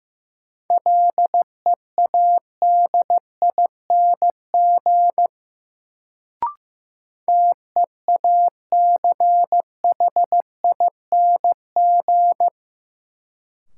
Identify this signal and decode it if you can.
Morse code, heard as a keyed tone, about 16 wpm, 700 Hz. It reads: LEADING TEACHING
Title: Sentences from Top 1100 Words - New Word Drill - 15wpm